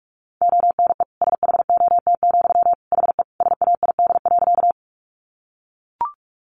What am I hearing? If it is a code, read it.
Morse code, heard as a keyed tone, about 44 wpm, 700 Hz. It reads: ONE SHOT, HE SAID.
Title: Sentences from Top 1100 Words 45wpm